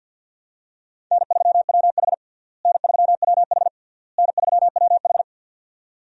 Morse code, heard as a keyed tone, about 50 wpm, 700 Hz. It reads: N3WH N3WH N3WH
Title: USA Call Signs - 4 Characters - Repeated 3x - 50wpm